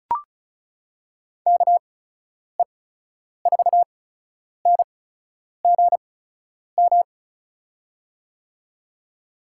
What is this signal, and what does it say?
Morse code, heard as a keyed tone, about 36 wpm, spaced as for 10 wpm, 700 Hz. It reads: KE4NGM
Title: USA Call Signs - 6 Characters - 8x Character Spacing - 35wpm